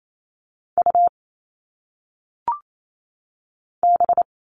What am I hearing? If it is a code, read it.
Morse code, heard as a keyed tone, about 28 wpm, 700 Hz. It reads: U B
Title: All Cut Numbers - 1-Digit 28wpm